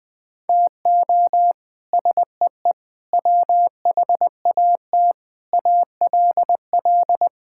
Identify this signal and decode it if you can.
Morse code, heard as a keyed tone, about 20 wpm, 700 Hz. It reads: TO SEE WHAT ALL
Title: Sets of 8 Words - 20wpm